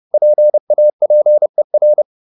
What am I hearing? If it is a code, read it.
Morse code, heard as a keyed tone, about 30 wpm, 600 Hz. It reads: PAPER